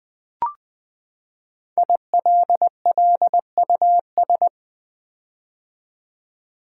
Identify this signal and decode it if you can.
Morse code, heard as a keyed tone, about 20 wpm, 700 Hz. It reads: ILLUS
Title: Build A Word - 5 Letters At A Time - 12-Letter Words - 20wpm